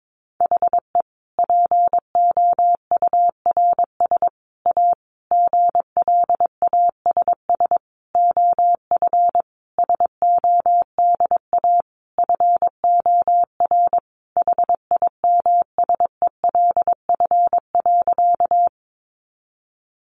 Morse code, heard as a keyed tone, about 22 wpm, 700 Hz. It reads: HE POURS A GLASS OF SODA FOR HIMSELF.
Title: Additional Easy Phrases 22wpm